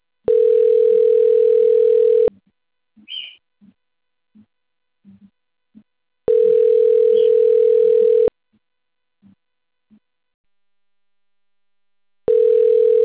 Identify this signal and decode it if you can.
Ringback tone (North America), 440 + 480 Hz.